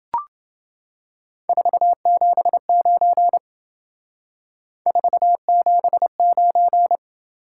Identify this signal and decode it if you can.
Morse code, heard as a keyed tone, about 28 wpm, 700 Hz. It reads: 479 479